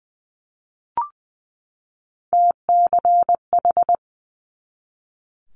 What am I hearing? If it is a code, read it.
Morse code, heard as a keyed tone, about 19 wpm, 700 Hz. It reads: TCH
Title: Multi-letter Phonemes 20wpm